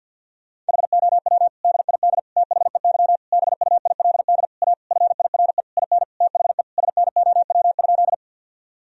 Morse code, heard as a keyed tone, about 50 wpm, 700 Hz. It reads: HOW DID THEY BUILD A FIRE IN THE SNOW?
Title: Sentences from Top 1000 Words - Review Part 3 - 50wpm